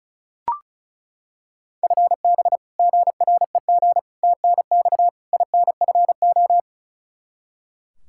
Morse code, heard as a keyed tone, about 36 wpm, 700 Hz. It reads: FB GREG TNX INFO
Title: Practice QSO - Exchange 3 - More Info - 35wpm